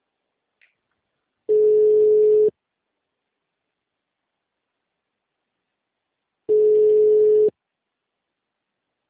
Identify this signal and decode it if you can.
Ringback tone (Europe, 425 Hz).